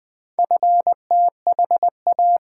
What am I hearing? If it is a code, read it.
Morse code, heard as a keyed tone, about 20 wpm, 700 Hz. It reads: FTHA
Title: Four Letter-Number Combinations - 20wpm